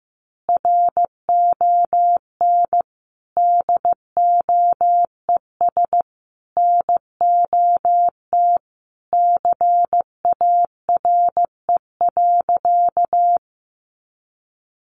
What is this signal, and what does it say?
Morse code, heard as a keyed tone, about 15 wpm, 700 Hz. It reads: RON DOES NOT CARE.
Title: Additional Easy Phrases 15wpm